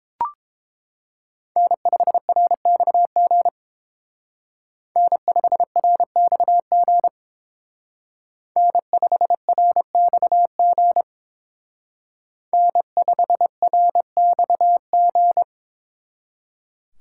Morse code, heard as a keyed tone, about 27 wpm, 700 Hz. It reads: N5RXG N5RXG N5RXG N5RXG
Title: Speed Racing - USA 1x3 Call Signs 22wpm